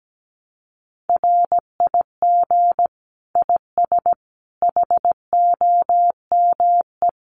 Morse code, heard as a keyed tone, about 17 wpm, 700 Hz. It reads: RIG IS HOME
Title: Common QSO Phrases 17wpm